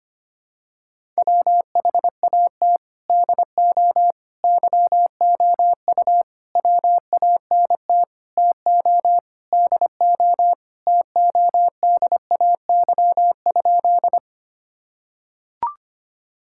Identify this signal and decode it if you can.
Morse code, heard as a keyed tone, about 25 wpm, 700 Hz. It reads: WHAT DO YOU WANT TO DO TODAY?
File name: practice / 25wpm / Sentences with 4-Word Phrases - 25wpm